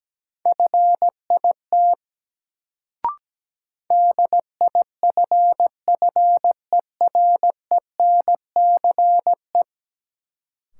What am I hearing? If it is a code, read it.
Morse code, heard as a keyed tone, about 17 wpm, 700 Hz. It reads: FIT DIFFERENCE
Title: Sentences from Top 800 Words - New Word Drill - 17wpm